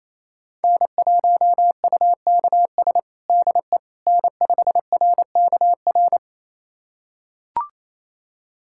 Morse code, heard as a keyed tone, about 28 wpm, 700 Hz. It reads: N1UKS DE N5RKR
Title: USA Callsign DE Callsign - 5 Characters - 28wpm